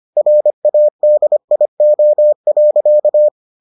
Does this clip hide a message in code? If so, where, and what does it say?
Yes: Morse code from 0.2 to 3.3 s, heard as a keyed tone, about 25 wpm, 600 Hz. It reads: RADIO.